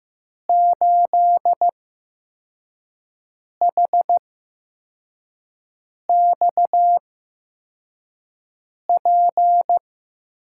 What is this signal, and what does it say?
Morse code, heard as a keyed tone, about 15 wpm, spaced as for 4 wpm, 700 Hz. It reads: 8HXP
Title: USA Call Signs - 6 Characters - 8x Character Spacing - 15wpm